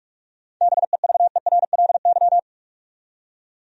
Morse code, heard as a keyed tone, about 43 wpm, 700 Hz. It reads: BEVERLY
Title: USA Common Names - 45wpm